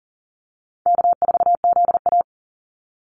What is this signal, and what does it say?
Morse code, heard as a keyed tone, about 40 wpm, 700 Hz. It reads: K4ZA
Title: Easy Call Signs Repeated 2x - 40wpm